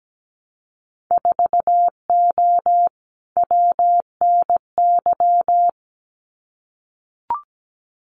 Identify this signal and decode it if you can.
Morse code, heard as a keyed tone, about 17 wpm, 700 Hz. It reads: 4O WNY
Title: Winter Field Day Exchanges Encore 17wpm